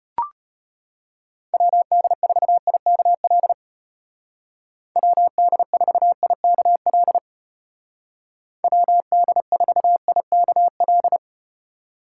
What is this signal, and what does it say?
Morse code, heard as a keyed tone, about 34 wpm, 700 Hz. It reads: WD4IKL WD4IKL WD4IKL